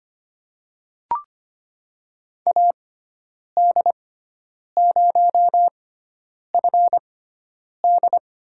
Morse code, heard as a keyed tone, about 25 wpm, spaced as for 9 wpm, 690 Hz. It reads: AD0FD
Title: USA Call Signs - 5 Characters - 6x Character Spacing - 25wpm